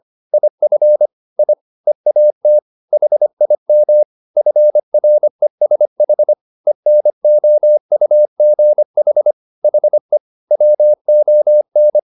Morse code, heard as a keyed tone, about 25 wpm, 600 Hz. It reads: IF I EAT HIM FRESH ENOUGH HE WON
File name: the old man and the sea part 1 25wpm